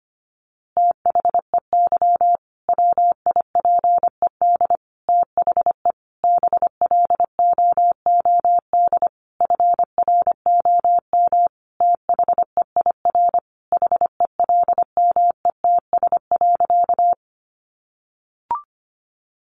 Morse code, heard as a keyed tone, about 25 wpm, 700 Hz. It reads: THEY WIPED THE BLOOD FROM THEIR HELMETS.
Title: A Sound Of Thunder 25wpm